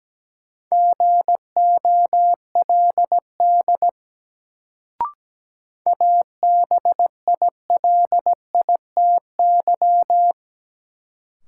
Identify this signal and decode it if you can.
Morse code, heard as a keyed tone, about 17 wpm, 700 Hz. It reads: GOLD ABILITY
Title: Sentences from Top 1000 Words - Review Part 3 - New Word Drill - 17wpm